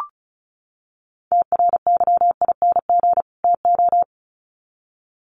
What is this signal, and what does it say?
Morse code, heard as a keyed tone, about 34 wpm, 700 Hz. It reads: TRYING TO
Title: Sets of 2 Words 35wpm